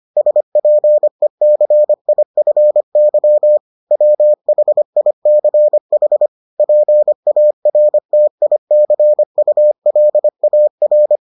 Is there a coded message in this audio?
Yes: Morse code from 0.2 to 11.2 s, heard as a keyed tone, about 25 wpm, 600 Hz. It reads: SPECIFY WHICH PARTICULAR